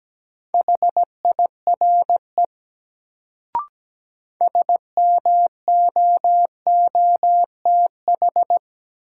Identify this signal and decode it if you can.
Morse code, heard as a keyed tone, about 17 wpm, 700 Hz. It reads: HIRE SMOOTH
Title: Sentences from Top 1400 Words - New Word Drill Alt - 17wpm